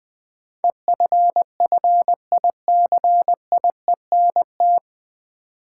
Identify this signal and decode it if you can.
Morse code, heard as a keyed tone, about 21 wpm, 700 Hz. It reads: EFFICIENT
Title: Sentences from Top 1300 Words - New Word Drill - 20wpm